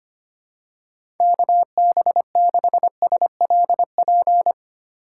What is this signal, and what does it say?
Morse code, heard as a keyed tone, about 25 wpm, 700 Hz. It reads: KB6SLP
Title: USA Call Signs - 6 Characters - 4x Character Spacing - 25wpm